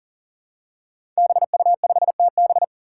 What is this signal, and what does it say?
Morse code, heard as a keyed tone, about 40 wpm, 700 Hz. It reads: BU5TB
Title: Five Challenging Letter-Number Combinations 40wpm